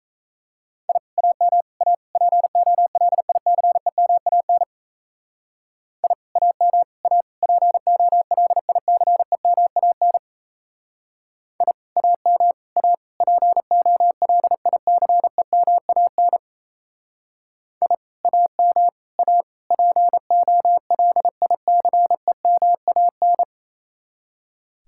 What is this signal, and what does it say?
Morse code, heard as a keyed tone, about 33 wpm, 700 Hz. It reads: I AM A POLICEMAN I AM A POLICEMAN I AM A POLICEMAN I AM A POLICEMAN
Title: Speed-Racing Common QSO Phrases - 28wpm